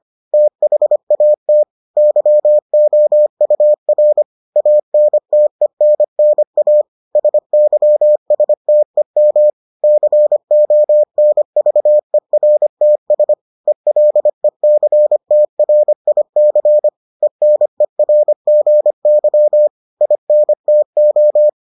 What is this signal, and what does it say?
Morse code, heard as a keyed tone, about 25 wpm, 600 Hz. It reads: THAT YOUR ANTENNA SYSTEM CONVERTS ELECTRIC ENERGY INTO